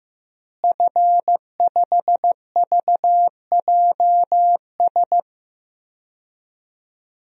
Morse code, heard as a keyed tone, about 15 wpm, 700 Hz. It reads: F5VJS
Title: DXCC-100 Call Signs - 5 Characters - Repeated 2x - 15wpm